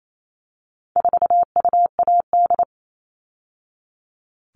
Morse code, heard as a keyed tone, about 28 wpm, 700 Hz. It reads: 4UAD
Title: All Cut Numbers - 4-Digit 28wpm